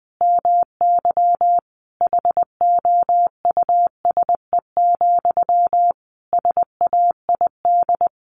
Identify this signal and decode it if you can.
Morse code, heard as a keyed tone, about 20 wpm, 700 Hz. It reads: MY HOUSE, SAID
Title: Sentences from Top 400 Words 20wpm